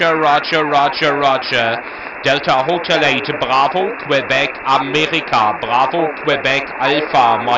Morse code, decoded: none